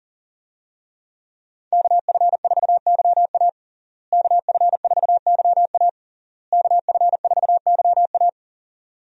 Morse code, heard as a keyed tone, about 40 wpm, 700 Hz. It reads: KF4YA KF4YA KF4YA
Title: USA Call Signs - 5 Characters - Repeated 3x - 40wpm